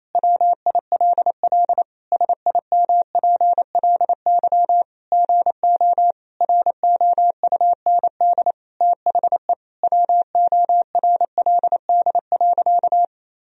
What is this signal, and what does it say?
Morse code, heard as a keyed tone, about 28 wpm, 700 Hz. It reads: WILL SIMPLY GO ROUND THE WORLD.
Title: Sentences from Top 1000 Words - Review Part 4 - 28wpm